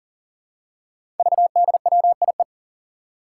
Morse code, heard as a keyed tone, about 37 wpm, 700 Hz. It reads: VDWIE
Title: Five Challenging Letter-Number Combinations 40wpm